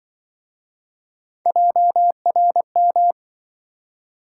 Morse code, heard as a keyed tone, about 26 wpm, 700 Hz. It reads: JRM